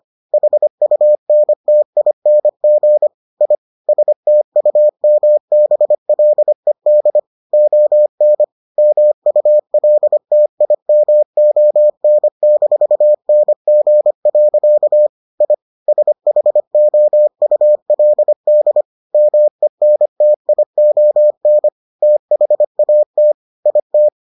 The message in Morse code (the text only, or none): HUNTING I STUMBLED ON MULTIMON-NG. I SHOULD MENTION THAT IT